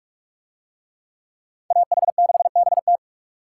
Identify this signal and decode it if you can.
Morse code, heard as a keyed tone, about 41 wpm, 700 Hz. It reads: AH6BT